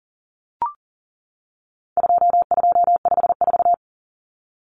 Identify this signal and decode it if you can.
Morse code, heard as a keyed tone, about 37 wpm, 700 Hz. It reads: 2254
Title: Common Cut Numbers - 4-Digit 40wpm